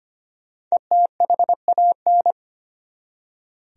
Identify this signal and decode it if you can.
Morse code, heard as a keyed tone, about 25 wpm, 700 Hz. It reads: ETHAN
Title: USA Common Names - 25wpm